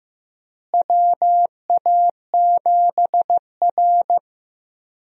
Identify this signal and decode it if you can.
Morse code, heard as a keyed tone, about 15 wpm, 700 Hz. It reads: WA7R
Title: USA Call Signs - 4 Characters - Repeated 2x - 15wpm